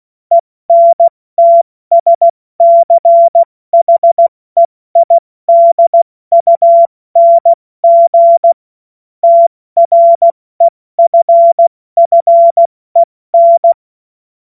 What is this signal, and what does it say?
Morse code, heard as a keyed tone, about 16 wpm, 680 Hz. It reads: ENTSCHEIDUNG TREFFEN
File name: Witz-16-September-2025-70-F80